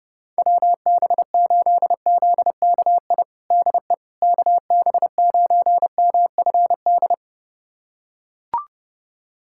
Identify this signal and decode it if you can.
Morse code, heard as a keyed tone, about 32 wpm, 700 Hz. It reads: WB8ZKI DE KB9MFD
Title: USA Callsign DE Callsign - 6 Characters - 30wpm